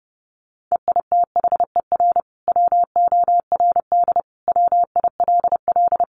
Morse code, heard as a keyed tone, about 30 wpm, 700 Hz. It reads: EITHER WORD WILL
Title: Sentences from Top 600 Words 30wpm